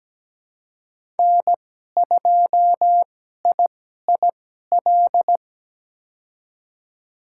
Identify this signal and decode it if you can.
Morse code, heard as a keyed tone, about 17 wpm, spaced as for 12 wpm, 700 Hz. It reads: N2IIL
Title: USA Call Signs - 5 Characters - 2x Character Spacing - 17wpm